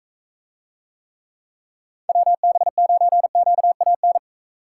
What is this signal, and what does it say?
Morse code, heard as a keyed tone, about 42 wpm, 700 Hz. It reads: WB9QAN